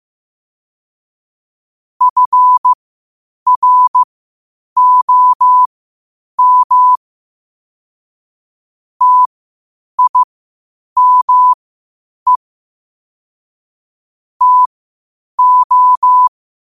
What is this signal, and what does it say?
Morse code, heard as a keyed tone, about 15 wpm, spaced as for 8 wpm, 1000 Hz. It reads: FROM TIME TO
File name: qotdfiles / 8wpm